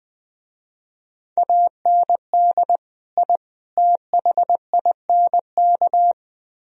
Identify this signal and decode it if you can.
Morse code, heard as a keyed tone, about 19 wpm, 700 Hz. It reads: AND I THINK
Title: Speed-Racing - Sets of 3 Words 17wpm